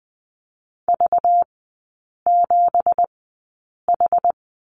Morse code, heard as a keyed tone, about 20 wpm, spaced as for 8 wpm, 690 Hz. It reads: V7H
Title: 191 Lesson 31e - Letter v - Three Characters 20wpm